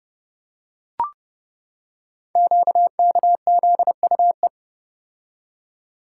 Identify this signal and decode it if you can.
Morse code, heard as a keyed tone, about 28 wpm, 690 Hz. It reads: QKZUE
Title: Code Groups - 5 Letters - 30wpm